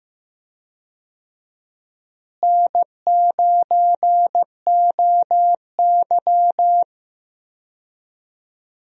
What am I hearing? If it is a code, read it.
Morse code, heard as a keyed tone, about 14 wpm, 700 Hz. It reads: N9OY
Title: USA Call Signs - 4 Characters - Repeated 2x - 15wpm